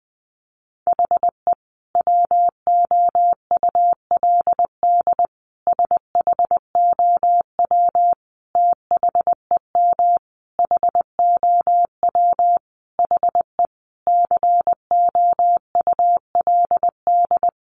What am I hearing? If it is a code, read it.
Morse code, heard as a keyed tone, about 20 wpm, 700 Hz. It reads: HE WOULD SHOW THEM HOW HE COULD
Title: Sentences from Top 1300 Words 20wpm